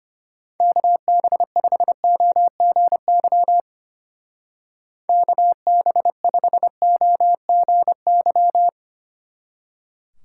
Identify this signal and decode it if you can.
Morse code, heard as a keyed tone, about 32 wpm, 700 Hz. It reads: KB5OGY KB5OGY